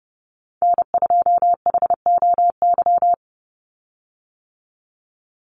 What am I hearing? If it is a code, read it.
Morse code, heard as a keyed tone, about 30 wpm, 700 Hz. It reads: N2HOY
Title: USA Call Signs 30wpm